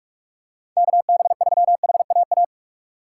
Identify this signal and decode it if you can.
Morse code, heard as a keyed tone, about 44 wpm, 700 Hz. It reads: KB3HAA